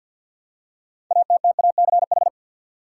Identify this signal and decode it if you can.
Morse code, heard as a keyed tone, about 51 wpm, 700 Hz. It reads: ATTACH
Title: Six Letter Words - 50wpm